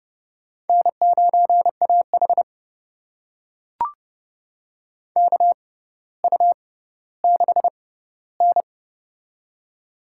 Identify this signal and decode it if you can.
Morse code, heard as a keyed tone, about 29 wpm, spaced as for 15 wpm, 690 Hz. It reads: N9AH KU6N
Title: USA Call Signs - 4 Characters - 6x Character Spacing - 30wpm